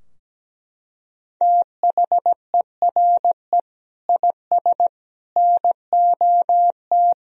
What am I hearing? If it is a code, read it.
Morse code, heard as a keyed tone, about 17 wpm, 700 Hz. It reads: THERE IS NOT
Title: Sentences from Top 1000 Words - Review Part 4 - 17wpm